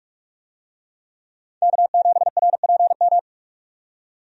Morse code, heard as a keyed tone, about 43 wpm, 700 Hz. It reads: K7RPM
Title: USA Call Signs - 5 Characters - 45wpm